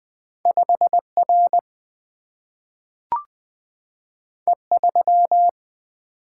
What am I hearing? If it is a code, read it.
Morse code, heard as a keyed tone, about 20 wpm, 700 Hz. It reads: 5R E3
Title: Single-Letter Single-Number 20wpm